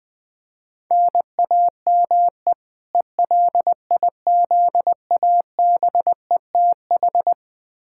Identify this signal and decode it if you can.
Morse code, heard as a keyed tone, about 20 wpm, 700 Hz. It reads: NAME ELIZABETH
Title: K1USN SST QSO Practice 20wpm